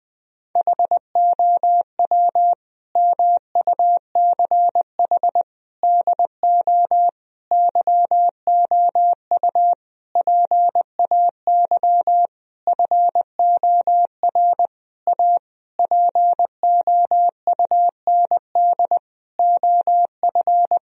Morse code, heard as a keyed tone, about 20 wpm, 700 Hz. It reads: HOW MUCH DO YOU PAY FOR A POUND OF
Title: Sentences from Top 1000 Words - Review Part 3 - 20wpm